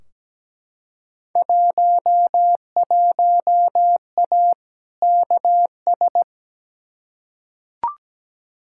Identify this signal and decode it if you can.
Morse code, heard as a keyed tone, about 17 wpm, 700 Hz. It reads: 11A KS
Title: Field Day Exchanges Encore - 17wpm